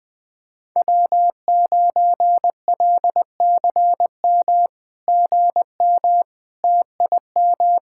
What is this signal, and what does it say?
Morse code, heard as a keyed tone, about 20 wpm, 700 Hz. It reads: W9LCM GM TIM
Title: SOTA QSO Practice 20wpm